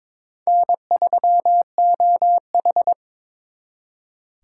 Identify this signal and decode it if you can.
Morse code, heard as a keyed tone, about 26 wpm, 700 Hz. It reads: N3OH
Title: Speed Racing - USA 1x2 Call Signs 22wpm